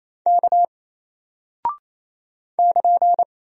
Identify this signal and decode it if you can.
Morse code, heard as a keyed tone, about 28 wpm, 700 Hz. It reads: K <KN>
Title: Single Character (Letter:Number:Punctuation:Prosign) Alt 28wpm